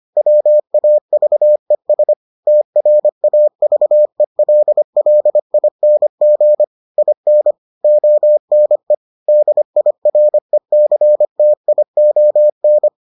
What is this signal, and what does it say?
Morse code, heard as a keyed tone, about 25 wpm, 600 Hz. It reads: WAVES TRAVELLING IN ONE DIRECTION